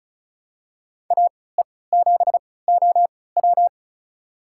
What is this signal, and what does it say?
Morse code, heard as a keyed tone, about 36 wpm, spaced as for 20 wpm, 700 Hz. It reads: AE7OW
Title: USA Call Signs - 5 Characters - 3x Character Spacing - 35wpm